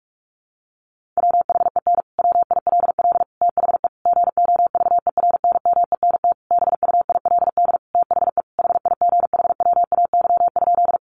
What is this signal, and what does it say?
Morse code, heard as a keyed tone, about 44 wpm, 700 Hz. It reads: WHEN WILL THE GOVERNMENT BUILD THE HIGHWAY?
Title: Sentences from Top 1400 Words 45wpm